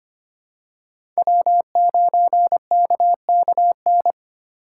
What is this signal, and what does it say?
Morse code, heard as a keyed tone, about 25 wpm, 700 Hz. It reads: W9KKN
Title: USA Call Signs - 4x Character Spacing - 25wpm